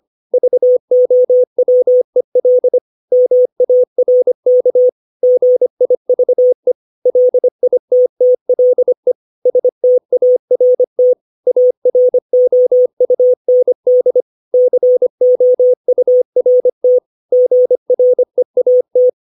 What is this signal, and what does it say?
Morse code, heard as a keyed tone, about 25 wpm, 500 Hz. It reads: VOWEL MARK GIVE LITTLE START AROUND COURT GREAT
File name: 632 words / 632 Words 25WPM